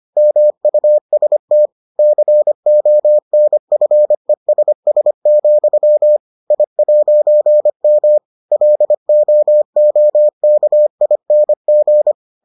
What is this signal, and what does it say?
Morse code, heard as a keyed tone, about 25 wpm, 600 Hz. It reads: MUST CONFESS, I'M LOOKING